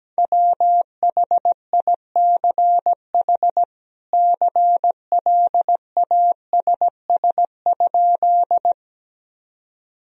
Morse code, heard as a keyed tone, about 17 wpm, 700 Hz. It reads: WHICH CLASS?
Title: Sentences from Top 1000 Words - Review Part 2 - 17wpm